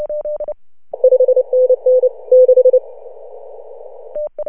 Morse code, heard as a keyed tone, about 29 wpm, 510 Hz. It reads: 5NN 6
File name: CW / HA2OS RHOB 2017 CW